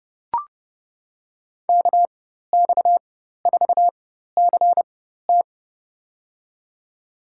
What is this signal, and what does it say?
Morse code, heard as a keyed tone, about 30 wpm, spaced as for 14 wpm, 700 Hz. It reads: KX4CT
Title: USA Call Signs - 5 Characters - 4x Character Spacing - 30wpm